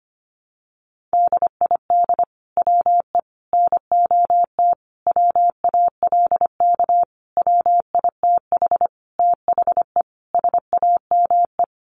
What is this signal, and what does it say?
Morse code, heard as a keyed tone, about 25 wpm, 700 Hz. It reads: DID WE NOT WALK WITH THE SAME
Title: Sentences from Top 1400 Words 25wpm